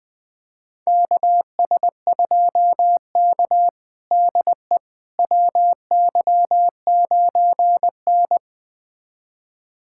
Morse code, heard as a keyed tone, about 20 wpm, 700 Hz. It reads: KS2K DE WY9N
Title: USA Callsign DE Callsign - 4 Characters - 20wpm